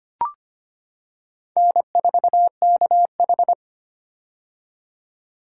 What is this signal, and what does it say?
Morse code, heard as a keyed tone, about 25 wpm, 700 Hz. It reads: N4KH
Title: K1USN SST QSO Practice - 25wpm